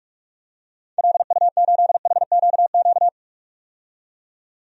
Morse code, heard as a keyed tone, about 46 wpm, 700 Hz. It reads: LU8HQQ
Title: DXCC-100 Call Signs - 6 Characters - 45wpm